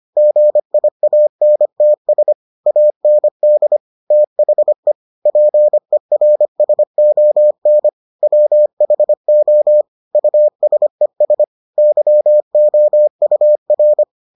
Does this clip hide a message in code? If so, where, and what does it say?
Yes: Morse code from 0.2 to 14.0 s, heard as a keyed tone, about 25 wpm, 600 Hz. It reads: GIANTS AND THE PERSON WHO USES YOUR